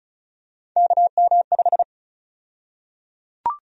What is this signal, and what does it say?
Morse code, heard as a keyed tone, about 34 wpm, 700 Hz. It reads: KM5
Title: USA Call Sign Prefixes - 35wpm